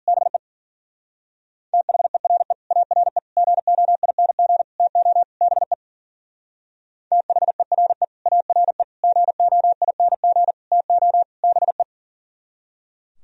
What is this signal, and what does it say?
Morse code, heard as a keyed tone, about 49 wpm, 700 Hz. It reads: BE THERE ARE GOING TO BE THERE ARE GOING TO BE
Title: Speed-Racing - Sets of 5 Words 40wpm